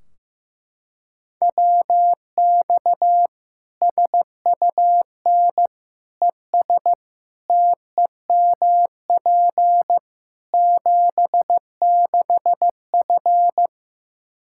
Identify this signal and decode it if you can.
Morse code, heard as a keyed tone, about 15 wpm, 700 Hz. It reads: WX SUN ES TEMP 76F
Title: K1USN SST QSO Practice 15wpm